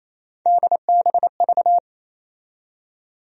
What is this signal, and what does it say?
Morse code, heard as a keyed tone, about 28 wpm, 700 Hz. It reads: DBV
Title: USA Call Sign Postfixes - 3 Letters 28wpm